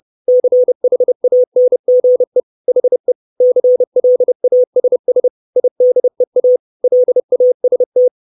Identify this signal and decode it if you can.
Morse code, heard as a keyed tone, about 30 wpm, 500 Hz. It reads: CHANGE HE CLASS IDEA LAST
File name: 632 Words 30WPM